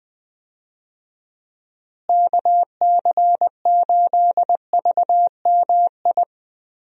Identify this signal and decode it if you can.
Morse code, heard as a keyed tone, about 20 wpm, 700 Hz. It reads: KC8VMI
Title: USA Call Signs - 6 Characters - Repeated 3x - 20wpm